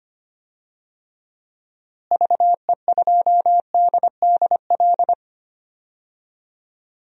Morse code, heard as a keyed tone, about 25 wpm, 700 Hz. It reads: VE2DDL
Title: DXCC-100 Call Signs - 6 Characters - Repeated 2x - 25wpm